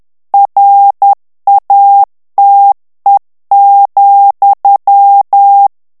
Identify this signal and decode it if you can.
Morse code, heard as a keyed tone, about 10 wpm, 800 Hz. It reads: RATE,
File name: ANGLAIS texte clair 3 vitesse 10 mots minute